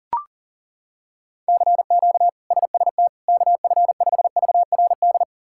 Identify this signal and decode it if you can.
Morse code, heard as a keyed tone, about 40 wpm, 700 Hz. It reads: CQ SST KF5VRD
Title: K1USN SST QSO Practice - 40wpm